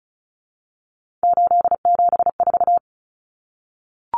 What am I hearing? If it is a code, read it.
Morse code, heard as a keyed tone, about 34 wpm, 700 Hz. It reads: 874
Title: Numbers - 3-Digit 35wpm